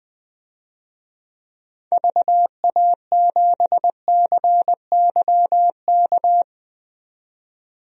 Morse code, heard as a keyed tone, about 20 wpm, 700 Hz. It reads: VA7CYK
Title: DXCC-100 Call Signs - 6 Characters - 20wpm